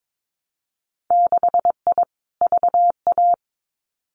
Morse code, heard as a keyed tone, about 22 wpm, 700 Hz. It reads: 6I VA
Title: Winter Field Day Exchanges Encore 22wpm